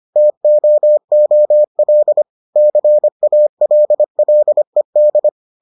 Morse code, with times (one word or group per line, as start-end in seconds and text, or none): 0.2-2.2 TOOL
2.6-5.3 CALLED